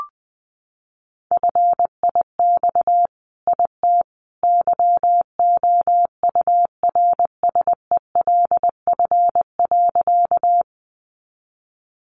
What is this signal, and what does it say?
Morse code, heard as a keyed tone, about 20 wpm, 700 Hz. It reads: FIX IT YOURSELF.
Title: Sentences from Top 1400 Words 20wpm